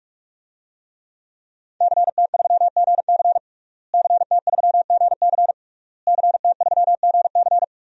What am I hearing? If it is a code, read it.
Morse code, heard as a keyed tone, about 46 wpm, 700 Hz. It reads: CT3GC CT3GC CT3GC
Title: DXCC-100 Call Signs - 5 Characters - Repeated 3x - 45wpm